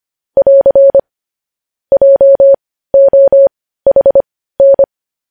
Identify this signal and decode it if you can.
Morse code, heard as a keyed tone, about 25 wpm, spaced as for 15 wpm, 550 Hz. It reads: <AR> JOHN
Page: news-headlines-in-morse-code-at-15-wpm